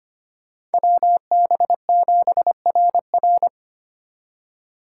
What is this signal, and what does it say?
Morse code, heard as a keyed tone, about 30 wpm, 700 Hz. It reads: WB7RR
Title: Speed Racing - USA 2x2 Call Signs 25wpm